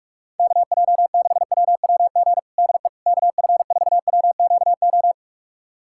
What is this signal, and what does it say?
Morse code, heard as a keyed tone, about 46 wpm, 700 Hz. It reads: KJ6WWG DE KF4WQO
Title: USA Callsign DE Callsign - 6 Characters - 45wpm